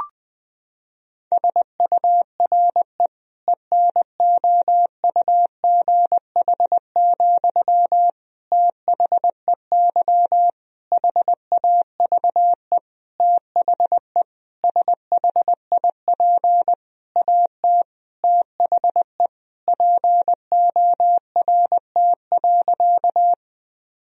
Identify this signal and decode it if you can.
Morse code, heard as a keyed tone, about 20 wpm, 700 Hz. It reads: SURE ENOUGH, THEY HAVE THE SHIP AT THE PORT.
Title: Sentences from Top 1000 Words - Review Part 2 - 20wpm